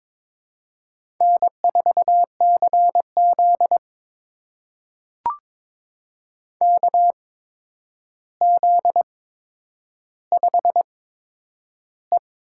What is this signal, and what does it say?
Morse code, heard as a keyed tone, about 22 wpm, spaced as for 9 wpm, 700 Hz. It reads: N4CZ KZ5E
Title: USA Call Signs - 8x Character Spacing - 22wpm